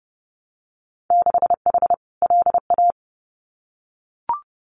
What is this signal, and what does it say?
Morse code, heard as a keyed tone, about 30 wpm, 700 Hz. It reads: 6H LA
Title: Winter Field Day Exchanges Encore 30wpm